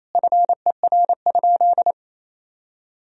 Morse code, heard as a keyed tone, about 28 wpm, 700 Hz. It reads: FER?
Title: Sentences from Top 100 Words and Abbreviations 28wpm